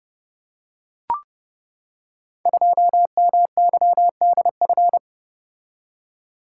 Morse code, heard as a keyed tone, about 30 wpm, 700 Hz. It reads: 2MYDF
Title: Code Groups - 5 Letter:Numbers - 30wpm